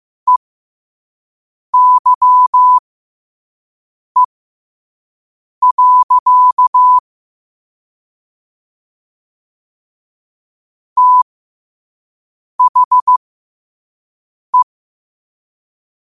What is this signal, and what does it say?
Morse code, heard as a keyed tone, about 15 wpm, spaced as for 5 wpm, 1000 Hz. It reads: EYE. THE
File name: qotdfiles / 5wpm